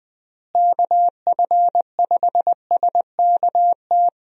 Morse code, heard as a keyed tone, about 20 wpm, 700 Hz. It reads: KF5SKT
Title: USA Call Signs - 6 Characters - Repeated 2x - 20wpm